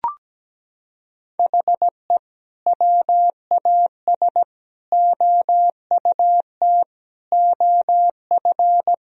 Morse code, heard as a keyed tone, about 17 wpm, 700 Hz. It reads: HE WAS OUT OF
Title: Sentences from Top 1000 Words - Review Part 4 - 17wpm